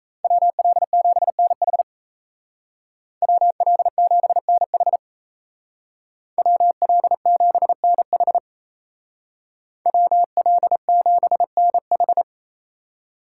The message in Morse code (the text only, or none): WL7NH WL7NH WL7NH WL7NH